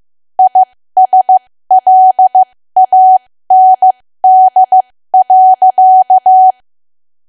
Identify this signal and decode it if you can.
Morse code, heard as a keyed tone, about 15 wpm, spaced as for 13 wpm, 750 Hz. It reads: ISLAND.
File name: cwa / W1AW-Level2 / 13WPM